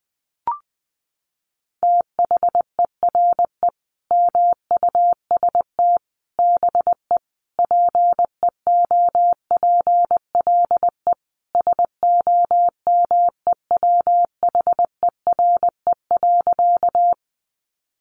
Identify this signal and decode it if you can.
Morse code, heard as a keyed tone, about 20 wpm, 700 Hz. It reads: THERE MUST BE PEOPLE SOMEWHERE.
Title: Sentences from Top 1400 Words 20wpm